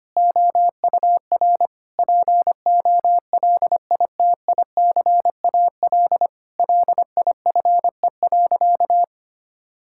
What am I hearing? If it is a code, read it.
Morse code, heard as a keyed tone, about 25 wpm, 700 Hz. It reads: OUR POLITICAL LIFE.
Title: Sentences from Top 1000 Words - Review Part 5 - 25wpm